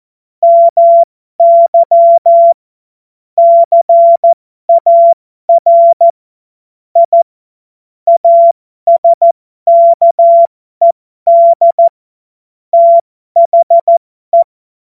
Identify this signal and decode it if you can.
Morse code, heard as a keyed tone, about 14 wpm, spaced as for 12 wpm, 680 Hz. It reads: MY CAR I ASKED THE
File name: Morsekurs / Witze